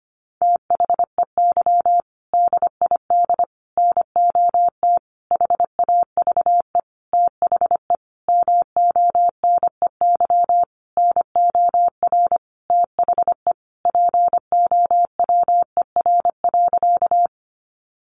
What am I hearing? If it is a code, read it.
Morse code, heard as a keyed tone, about 25 wpm, 700 Hz. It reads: THEY DID NOT HAVE THE MONEY NOR THE POWER.
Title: Sentences from Top 1500 Words 25wpm